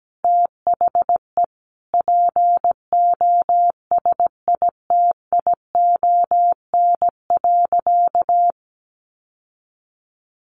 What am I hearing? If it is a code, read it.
Morse code, heard as a keyed tone, about 17 wpm, 700 Hz. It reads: THE POSITION.
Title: Sentences from Top 1400 Words 17wpm